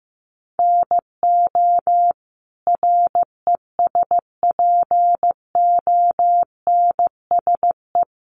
Morse code, heard as a keyed tone, about 15 wpm, 700 Hz. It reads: NO RESPONSE
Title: Sentences from Top 1400 Words 15wpm